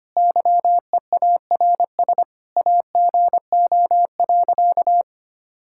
Morse code, heard as a keyed tone, about 25 wpm, 700 Hz. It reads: YEARS AGO.
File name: practice / 25wpm / Sentences with 4-Word Phrases - 25wpm